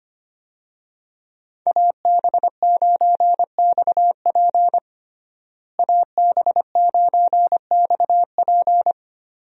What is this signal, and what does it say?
Morse code, heard as a keyed tone, about 25 wpm, 700 Hz. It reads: AB9XP AB9XP
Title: USA Call Signs - 5 Characters - Repeated 2x - 25wpm